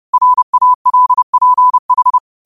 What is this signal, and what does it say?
Morse code, heard as a keyed tone, about 30 wpm, 1000 Hz. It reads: RALPH